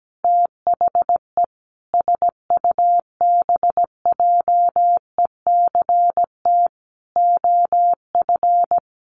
Morse code, heard as a keyed tone, about 17 wpm, 700 Hz. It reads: THE SUBJECT OF
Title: Sentences from Top 1100 Words 17wpm